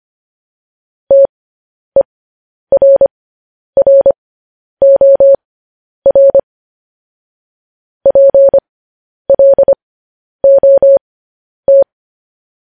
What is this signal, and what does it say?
Morse code, heard as a keyed tone, about 25 wpm, spaced as for 10 wpm, 550 Hz. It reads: TERROR PLOT
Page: news-headlines-in-morse-code-at-10-wpm